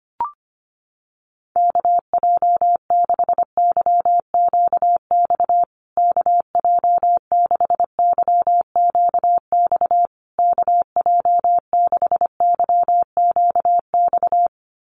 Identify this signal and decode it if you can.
Morse code, heard as a keyed tone, about 25 wpm, 700 Hz. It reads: KJ6YQX KJ6YQX KJ6YQX
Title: USA Call Signs Repeated 3x 25wpm